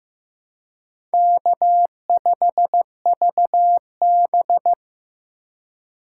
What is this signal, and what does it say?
Morse code, heard as a keyed tone, about 18 wpm, 700 Hz. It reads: K5VB
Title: Speed Racing - USA 1x2 Call Signs 15wpm